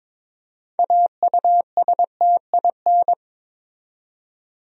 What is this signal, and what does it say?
Morse code, heard as a keyed tone, about 22 wpm, 700 Hz. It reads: AUSTIN
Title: USA Common Names - 22wpm